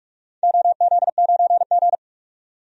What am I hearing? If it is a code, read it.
Morse code, heard as a keyed tone, about 46 wpm, 700 Hz. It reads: OZ9G
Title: DXCC-100 Call Signs - 4 Characters - 45wpm